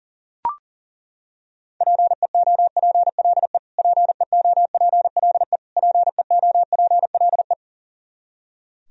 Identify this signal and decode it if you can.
Morse code, heard as a keyed tone, about 40 wpm, 700 Hz. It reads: PEOPLE PEOPLE PEOPLE
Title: Top 100 Words Repeated 40wpm